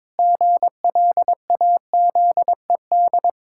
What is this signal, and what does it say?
Morse code, heard as a keyed tone, about 21 wpm, 700 Hz. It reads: GLAZED
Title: Pangram Challenge v2 - 22wpm